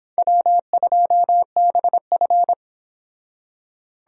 Morse code, heard as a keyed tone, about 25 wpm, 700 Hz. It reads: W2BF